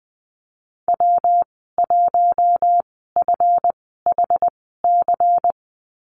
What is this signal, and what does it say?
Morse code, heard as a keyed tone, about 20 wpm, spaced as for 15 wpm, 700 Hz. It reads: W1FHC
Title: 179 Lesson 29g - Slash - Call Signs 20wpm